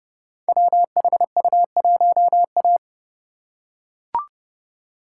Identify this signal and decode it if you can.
Morse code, heard as a keyed tone, about 30 wpm, 700 Hz. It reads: WHU1A
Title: Code Groups - 5 Letter:Numbers - 30wpm